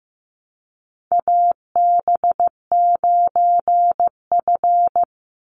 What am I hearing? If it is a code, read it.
Morse code, heard as a keyed tone, about 15 wpm, 700 Hz. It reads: AB9F
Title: USA Call Signs 15wpm